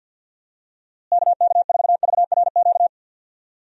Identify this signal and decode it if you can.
Morse code, heard as a keyed tone, about 51 wpm, 700 Hz. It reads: KK4VRQ
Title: USA Call Signs - 6 Characters - Repeated 3x - 50wpm